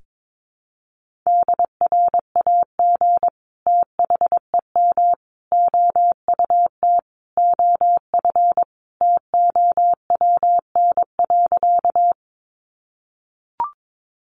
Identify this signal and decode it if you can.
Morse code, heard as a keyed tone, about 22 wpm, 700 Hz. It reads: DRAG THEM OUT OF TOWN.
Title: Sentences from Top 1500 Words 22wpm